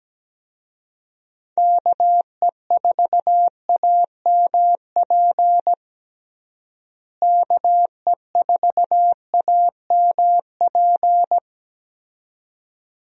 Morse code, heard as a keyed tone, about 17 wpm, 700 Hz. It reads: KE4AMP KE4AMP
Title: USA Call Signs - 6 Characters - Repeated 2x - 17wpm